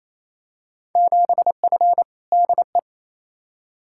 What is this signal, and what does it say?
Morse code, heard as a keyed tone, about 28 wpm, 700 Hz. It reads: 7F DE
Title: Field Day Exchanges Encore - 28wpm